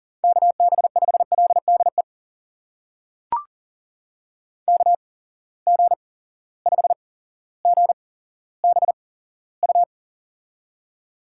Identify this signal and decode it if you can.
Morse code, heard as a keyed tone, about 38 wpm, spaced as for 17 wpm, 700 Hz. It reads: KB5LDE KG5GBU
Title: USA Call Signs - 8x Character Spacing - 40wpm